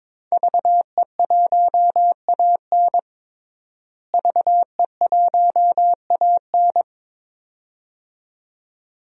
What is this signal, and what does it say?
Morse code, heard as a keyed tone, about 22 wpm, 700 Hz. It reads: VE1AN VE1AN
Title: DXCC-100 Call Signs - 5 Characters - Repeated 2x - 22wpm